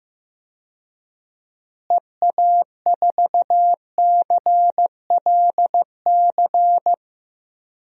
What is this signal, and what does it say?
Morse code, heard as a keyed tone, about 16 wpm, 700 Hz. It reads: EA4CLC
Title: DXCC-100 Call Signs - 6 Characters - Repeated 2x - 15wpm